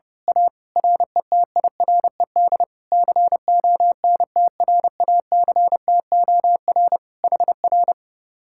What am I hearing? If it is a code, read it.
Morse code, heard as a keyed tone, about 30 wpm, 700 Hz. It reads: A RETIRED CONTRACTOR HR
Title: Speed-Racing Common QSO Phrases - 30wpm